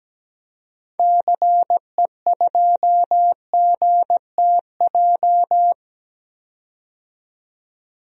Morse code, heard as a keyed tone, about 17 wpm, 700 Hz. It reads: CE2GTJ
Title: DXCC-100 Call Signs - 6 Characters - 17wpm